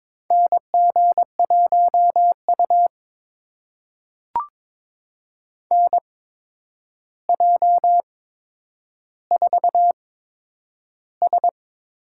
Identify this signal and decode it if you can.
Morse code, heard as a keyed tone, about 22 wpm, spaced as for 9 wpm, 700 Hz. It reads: NG1U NJ4S
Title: USA Call Signs - 4 Characters - 8x Character Spacing - 22wpm